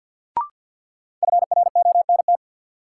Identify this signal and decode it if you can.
Morse code, heard as a keyed tone, about 47 wpm, 700 Hz. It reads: FRONT
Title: Sets of 2 Words - New Word Drill - 50wpm